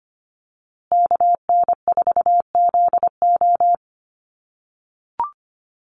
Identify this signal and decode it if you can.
Morse code, heard as a keyed tone, about 25 wpm, 700 Hz. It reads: KN4ZO
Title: USA Call Signs Repeated 2x 25wpm